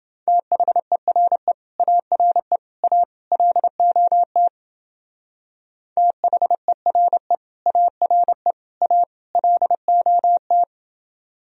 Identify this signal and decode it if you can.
Morse code, heard as a keyed tone, about 29 wpm, 700 Hz. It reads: THERE ARE A LOT THERE ARE A LOT